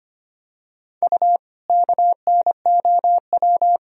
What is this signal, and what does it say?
Morse code, heard as a keyed tone, about 25 wpm, 700 Hz. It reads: U KNOW
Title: Sentences from Top 500 Words and Abbreviations 25wpm